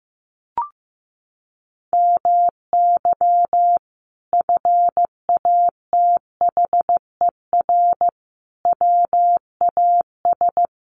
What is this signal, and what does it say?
Morse code, heard as a keyed tone, about 14 wpm, 700 Hz. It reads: MY FATHER WAS
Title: Sentences from Top 1400 Words 15wpm